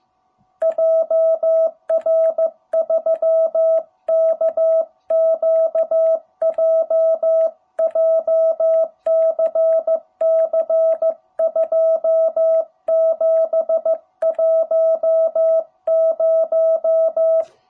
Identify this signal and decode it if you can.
Morse code, heard as a keyed tone, about 15 wpm, 660 Hz. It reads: JR3KQJ JCC 2710
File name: jh8cbh / 2021cw